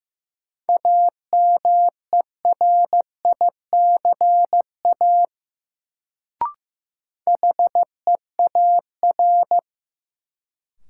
Morse code, heard as a keyed tone, about 15 wpm, 700 Hz. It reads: AMERICA HEAR
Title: Sets of 5 Words - New Word Drill - 15wpm